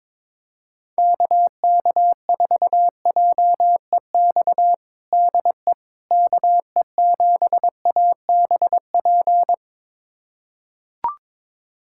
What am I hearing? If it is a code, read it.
Morse code, heard as a keyed tone, about 21 wpm, 700 Hz. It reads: KK4JEX DE KE7ABP
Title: USA Callsign DE Callsign - 6 Characters - 22wpm